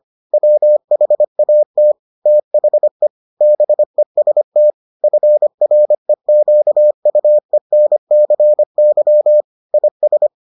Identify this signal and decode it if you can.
Morse code, heard as a keyed tone, about 25 wpm, 600 Hz. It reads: WHAT THE BEST FREQUENCY IS